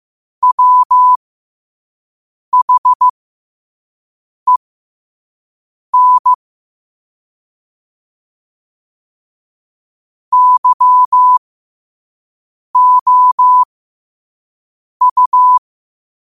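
Morse code, heard as a keyed tone, about 15 wpm, spaced as for 5 wpm, 1000 Hz. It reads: WHEN YOU
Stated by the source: Morse practice quote sent at 5 wpm.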